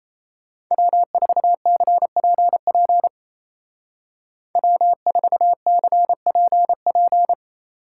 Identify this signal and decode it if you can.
Morse code, heard as a keyed tone, about 34 wpm, 700 Hz. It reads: W4CPP W4CPP
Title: Speed Racing - USA 1x3 Call Signs 28wpm